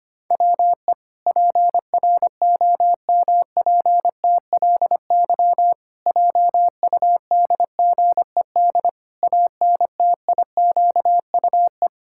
Morse code, heard as a keyed tone, about 25 wpm, 700 Hz. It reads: WE PROMPTLY JUDGED ANTIQUE
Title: Pangram Challenge v2 - 25wpm